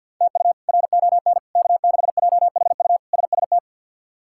Morse code, heard as a keyed tone, about 52 wpm, 700 Hz. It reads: TU RON K6JHU SST
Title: K1USN SST QSO Practice - 50wpm